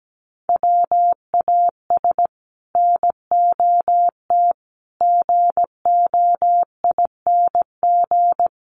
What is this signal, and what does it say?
Morse code, heard as a keyed tone, about 17 wpm, 700 Hz. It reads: WAS NOT GOING
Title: Sets of 5 Words Encore 17wpm